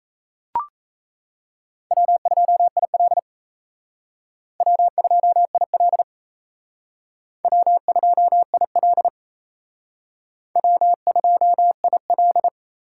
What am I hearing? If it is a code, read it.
Morse code, heard as a keyed tone, about 37 wpm, 700 Hz. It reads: W2IL W2IL W2IL W2IL